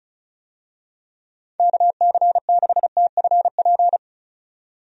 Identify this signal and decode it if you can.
Morse code, heard as a keyed tone, about 36 wpm, 700 Hz. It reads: KC6TFP
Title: USA Call Signs - 6 Characters - 35wpm